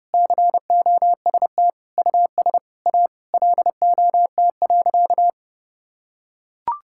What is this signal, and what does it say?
Morse code, heard as a keyed tone, about 30 wpm, 700 Hz. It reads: COST US A LOT.
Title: Sentences from Top 1000 Words - Review Part 5 - 30wpm